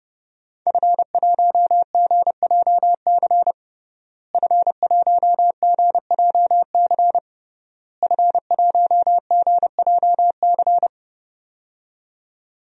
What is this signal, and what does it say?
Morse code, heard as a keyed tone, about 30 wpm, 700 Hz. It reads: F1GJC F1GJC F1GJC
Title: DXCC-100 Call Signs - 5 Characters - Repeated 3x - 30wpm